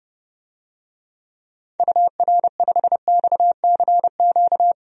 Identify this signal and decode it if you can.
Morse code, heard as a keyed tone, about 29 wpm, 690 Hz. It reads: UR5XCQ
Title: DXCC-100 Call Signs - 6 Characters - 30wpm